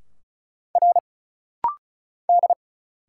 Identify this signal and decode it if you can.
Morse code, heard as a keyed tone, about 37 wpm, 700 Hz. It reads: R D
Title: Single Letters - Rapid-Fire 35wpm